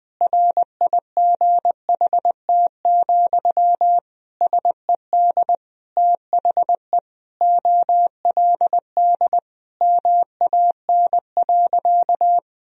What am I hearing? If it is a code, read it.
Morse code, heard as a keyed tone, about 20 wpm, 700 Hz. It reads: RIGHT, SED THE OLD MAN.
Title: Sentences from Top 200 Words and Abbreviations 20wpm